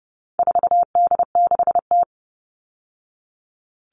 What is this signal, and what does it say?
Morse code, heard as a keyed tone, about 30 wpm, 700 Hz. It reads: 4D6T
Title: All Cut Numbers - 4-Digit 30wpm